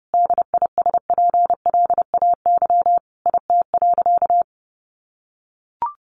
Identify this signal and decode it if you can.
Morse code, heard as a keyed tone, about 30 wpm, 700 Hz. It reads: DISPLAY IT.
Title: Sentences from Top 1200 Words 30wpm